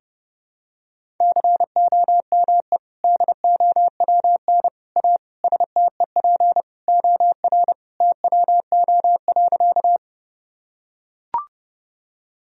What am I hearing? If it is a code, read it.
Morse code, heard as a keyed tone, about 30 wpm, 700 Hz. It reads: COME DOWN A STEP OR TWO.
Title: Sentences from Top 1000 Words - Review Part 2 - 30wpm